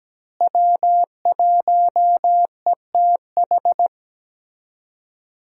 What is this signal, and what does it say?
Morse code, heard as a keyed tone, about 17 wpm, 700 Hz. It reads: W1ETH
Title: USA Call Signs - 5 Characters - Repeated 2x - 17wpm